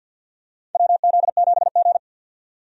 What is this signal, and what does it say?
Morse code, heard as a keyed tone, about 52 wpm, 700 Hz. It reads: WZ7G
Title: USA Call Signs - 4 Characters - 50wpm